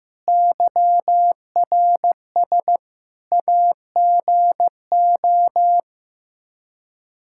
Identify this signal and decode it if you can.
Morse code, heard as a keyed tone, about 15 wpm, 700 Hz. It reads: YRS AGO
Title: Practice QSO - Exchange 3 - More Info - 15wpm